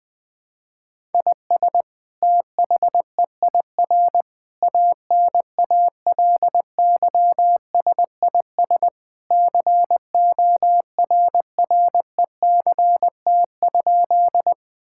Morse code, heard as a keyed tone, about 20 wpm, 700 Hz. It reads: IS THEIR ANALYSIS CORRECT?
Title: Sentences from Top 1000 Words - Review Part 5 - 20wpm